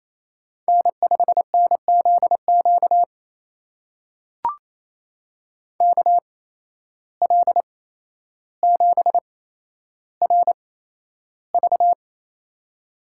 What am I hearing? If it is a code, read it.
Morse code, heard as a keyed tone, about 28 wpm, spaced as for 12 wpm, 700 Hz. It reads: N5NZQ KL7RV
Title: USA Call Signs - 5 Characters - 8x Character Spacing - 28wpm